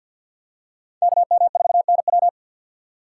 Morse code, heard as a keyed tone, about 46 wpm, 700 Hz. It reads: KM4NW
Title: USA Call Signs - 5 Characters - 2x Character Spacing - 50wpm